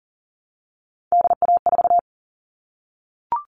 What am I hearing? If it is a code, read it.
Morse code, heard as a keyed tone, about 40 wpm, 700 Hz. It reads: DA4
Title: All Cut Numbers - 3-Digit 40wpm